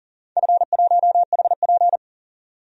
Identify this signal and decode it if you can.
Morse code, heard as a keyed tone, about 40 wpm, 700 Hz. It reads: F1HP
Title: DXCC-100 Call Signs - 4 Characters - 40wpm